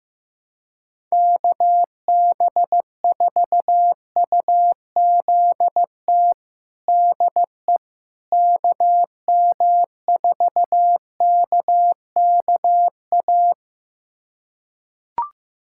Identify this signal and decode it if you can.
Morse code, heard as a keyed tone, about 16 wpm, 700 Hz. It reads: KB4UZT DE KM4KKA
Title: USA Callsign DE Callsign - 6 Characters - 15wpm